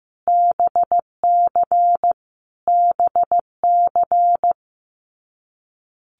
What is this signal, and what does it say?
Morse code, heard as a keyed tone, about 15 wpm, 700 Hz. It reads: BC BC
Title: Canadian Provinces and Territories - Repeated 3x - 15wpm